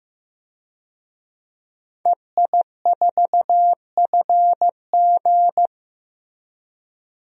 Morse code, heard as a keyed tone, about 15 wpm, 700 Hz. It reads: EI4FG
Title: DXCC-100 Call Signs - 5 Characters - 15wpm